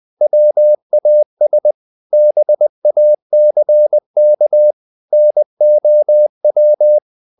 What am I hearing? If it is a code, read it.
Morse code, heard as a keyed tone, about 20 wpm, 600 Hz. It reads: WAS BACK NOW
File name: the old man and the sea / the old man and the sea part 1 20wpm